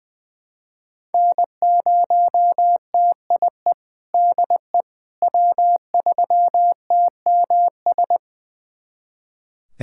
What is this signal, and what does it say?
Morse code, heard as a keyed tone, about 20 wpm, 700 Hz. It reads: N0TIE DE W3TMS
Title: USA Callsign DE Callsign - 5 Characters - 20wpm